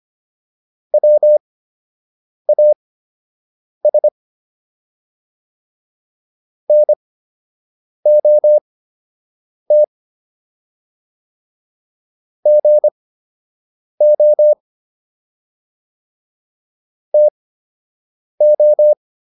Morse code, heard as a keyed tone, about 25 wpm, spaced as for 7 wpm, 600 Hz. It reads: WAS NOT GO TO